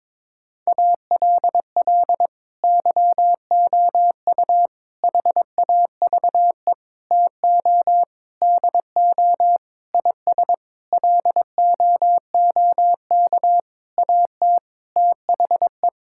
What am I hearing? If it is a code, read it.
Morse code, heard as a keyed tone, about 22 wpm, 700 Hz. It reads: ALL YOU HAVE TO DO IS LOOK AT THE
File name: practice / 22wpm / Sentences with 8-Word Phrases - 22wpm